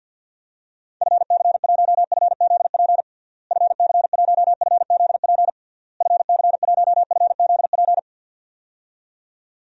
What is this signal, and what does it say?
Morse code, heard as a keyed tone, about 50 wpm, 700 Hz. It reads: FC1FZP FC1FZP FC1FZP
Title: DXCC-100 Call Signs - 6 Characters - Repeated 3x - 50wpm